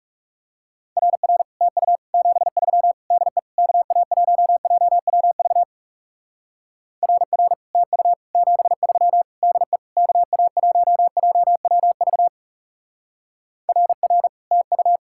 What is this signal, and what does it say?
Morse code, heard as a keyed tone, about 35 wpm, 700 Hz. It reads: RR TU 73 DE KA1JWV RR TU 73 DE KA1JWV RR TU
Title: Speed-Racing Common QSO Phrases Encore - 30wpm